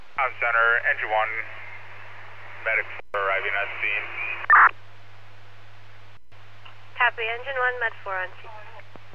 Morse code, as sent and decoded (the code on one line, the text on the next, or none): none